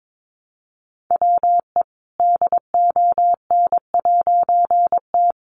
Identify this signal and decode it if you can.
Morse code, heard as a keyed tone, about 22 wpm, 700 Hz. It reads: WE DON'T
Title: A Sound Of Thunder 22wpm